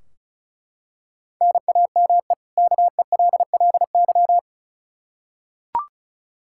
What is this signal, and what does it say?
Morse code, heard as a keyed tone, about 34 wpm, 700 Hz. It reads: NAME KELLY
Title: Practice QSO - Exchange 1 - Beginning - 35wpm